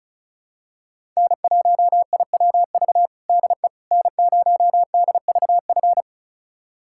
Morse code, heard as a keyed tone, about 36 wpm, 700 Hz. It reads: N1IWV DE N0DVF
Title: USA Callsign DE Callsign - 5 Characters - 35wpm